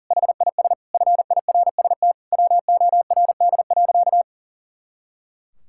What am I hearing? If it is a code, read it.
Morse code, heard as a keyed tone, about 40 wpm, 700 Hz. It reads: HIS FIRST WORD.
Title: Pangram Challenge v2 - 40wpm